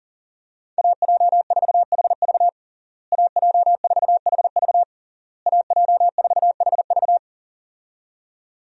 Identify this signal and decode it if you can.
Morse code, heard as a keyed tone, about 40 wpm, 700 Hz. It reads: AJ4HV AJ4HV AJ4HV
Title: USA Call Signs - 5 Characters - Repeated 3x - 40wpm